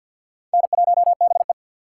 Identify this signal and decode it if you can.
Morse code, heard as a keyed tone, about 48 wpm, 700 Hz. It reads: N1BE
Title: USA Call Signs - 2x Character Spacing - 50wpm